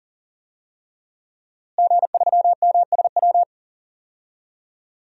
Morse code, heard as a keyed tone, about 40 wpm, 700 Hz. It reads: G3MSW
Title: DXCC-100 Call Signs - 5 Characters - Repeated 3x - 40wpm